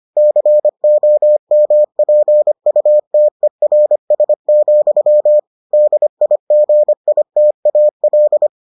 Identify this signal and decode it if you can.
Morse code, heard as a keyed tone, about 25 wpm, 600 Hz. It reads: COMPUTERS, DIGITAL